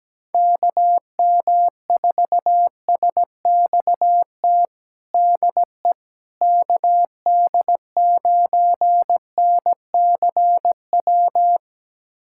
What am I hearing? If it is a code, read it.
Morse code, heard as a keyed tone, about 17 wpm, 700 Hz. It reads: KM4SXT DE KD9NCW
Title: USA Callsign DE Callsign - 6 Characters - 17wpm